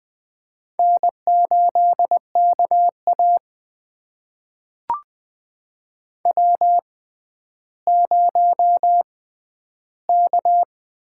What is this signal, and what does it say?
Morse code, heard as a keyed tone, about 20 wpm, spaced as for 11 wpm, 700 Hz. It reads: N8KA W0K
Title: USA Call Signs - 4 Characters - 6x Character Spacing - 20wpm